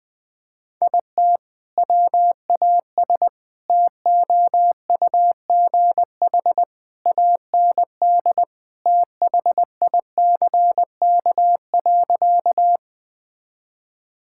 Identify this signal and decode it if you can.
Morse code, heard as a keyed tone, about 20 wpm, 700 Hz. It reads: IT WAS TOUGH AND THICK.
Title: Sentences from Top 1600 Words 20wpm